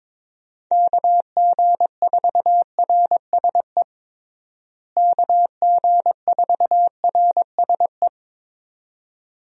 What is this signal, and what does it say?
Morse code, heard as a keyed tone, about 22 wpm, 700 Hz. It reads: KG4RSE KG4RSE
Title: USA Call Signs - 6 Characters - Repeated 2x - 22wpm